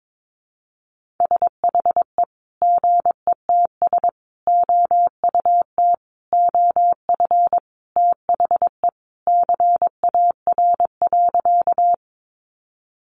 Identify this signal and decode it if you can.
Morse code, heard as a keyed tone, about 22 wpm, 700 Hz. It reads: SHE GETS OUT OF THE CAR.
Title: Easy Phrases 22wpm